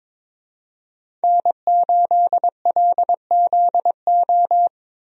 Morse code, heard as a keyed tone, about 22 wpm, 700 Hz. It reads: N8LZO
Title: USA Call Signs - 5 Characters - Repeated 3x - 22wpm